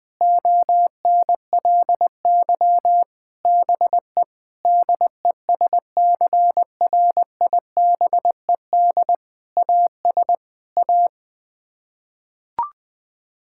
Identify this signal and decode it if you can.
Morse code, heard as a keyed tone, about 20 wpm, 700 Hz. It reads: ONLY BE DESCRIBED AS A
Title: Sets of 8 Words - 20wpm